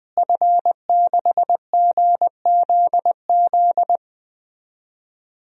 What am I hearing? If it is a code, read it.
Morse code, heard as a keyed tone, about 20 wpm, 700 Hz. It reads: F6GZZ
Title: DXCC-100 Call Signs - 5 Characters - Repeated 3x - 20wpm